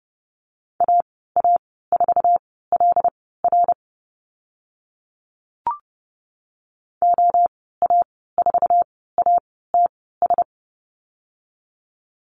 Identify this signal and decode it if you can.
Morse code, heard as a keyed tone, about 30 wpm, spaced as for 17 wpm, 700 Hz. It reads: AA4LR OA4ATS
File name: Lesson 14g - Number 2 - Call Signs 30wpm